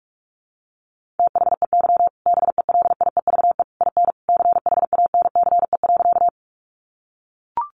Morse code, heard as a keyed tone, about 44 wpm, 700 Hz. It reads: THEY BELIEVE IN CHANCE.
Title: Sentences from Top 1000 Words 45wpm